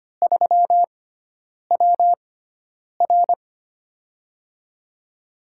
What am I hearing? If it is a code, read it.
Morse code, heard as a keyed tone, about 25 wpm, spaced as for 9 wpm, 700 Hz. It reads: 3WR
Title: USA Call Signs - 6x Character Spacing - 25wpm